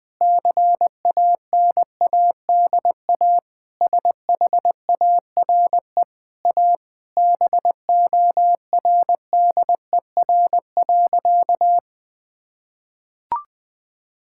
Morse code, heard as a keyed tone, about 20 wpm, 700 Hz. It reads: CANADA SHARE A BORDER.
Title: Sentences with 5-Word Phrases - 20wpm